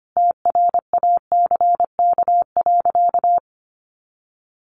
Morse code, heard as a keyed tone, about 25 wpm, 700 Hz. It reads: TRACK.
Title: Sentences from Top 1100 Words 25wpm